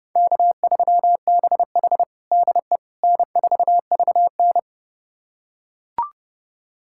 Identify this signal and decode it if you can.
Morse code, heard as a keyed tone, about 30 wpm, 700 Hz. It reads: K3BH DE N4VN
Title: USA Callsign DE Callsign - 4 Characters - 30wpm